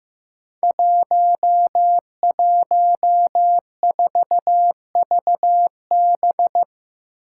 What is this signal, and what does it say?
Morse code, heard as a keyed tone, about 15 wpm, 700 Hz. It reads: 114VB
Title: Five Challenging Letter-Number Combinations 15wpm